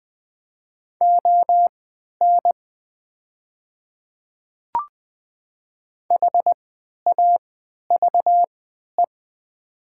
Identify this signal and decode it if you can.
Morse code, heard as a keyed tone, about 20 wpm, spaced as for 11 wpm, 700 Hz. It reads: ON HAVE
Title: 3x Character Spacing - Top 300 Words - 20wpm